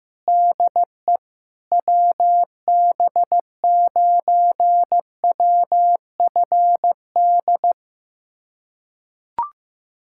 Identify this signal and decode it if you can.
Morse code, heard as a keyed tone, about 16 wpm, 700 Hz. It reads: DE WB9WFD
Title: USA Callsign DE Callsign 15wpm